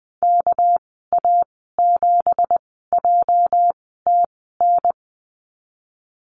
Morse code, heard as a keyed tone, about 20 wpm, spaced as for 15 wpm, 700 Hz. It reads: KA7JTN
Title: 213 Lesson 34g - Number 8 - Call Signs 20wpm